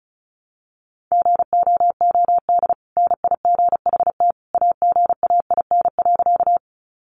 Morse code, heard as a keyed tone, about 34 wpm, 700 Hz. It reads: GOOD NIGHT AGAIN.
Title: Sentences from Top 300 Words 35wpm